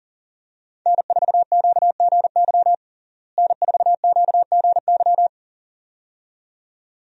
Morse code, heard as a keyed tone, about 40 wpm, 700 Hz. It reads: N4QGY N4QGY
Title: USA Call Signs - 5 Characters - Repeated 2x - 40wpm